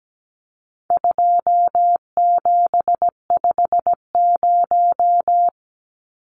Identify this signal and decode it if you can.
Morse code, heard as a keyed tone, about 17 wpm, 700 Hz. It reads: 2750
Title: Numbers - 4-Digit 17wpm